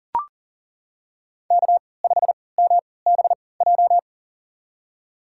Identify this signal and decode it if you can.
Morse code, heard as a keyed tone, about 40 wpm, spaced as for 23 wpm, 700 Hz. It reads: K5MBJ
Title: USA Call Signs - 5 Characters - 3x Character Spacing - 40wpm